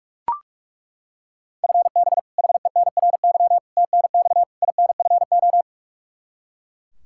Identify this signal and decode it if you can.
Morse code, heard as a keyed tone, about 41 wpm, 700 Hz. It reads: FB HENRY TNX INFO
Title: Practice QSO - Exchange 3 - More Info - 45wpm